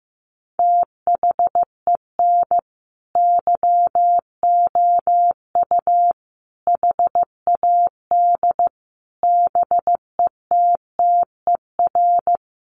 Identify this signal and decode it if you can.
Morse code, heard as a keyed tone, about 14 wpm, 700 Hz. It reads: THEN YOU HAD BETTER
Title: Sentences from Top 1300 Words 15wpm